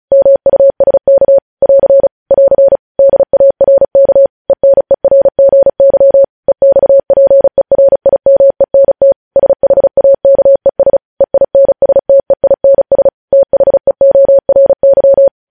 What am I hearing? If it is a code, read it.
Morse code, heard as a keyed tone, about 36 wpm, 550 Hz. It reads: MUSK <AR> <AR> DARK ENERGY EXPERIMENT SHAKES EINSTEINS THEORY
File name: podcast / News headlines-1x-35x35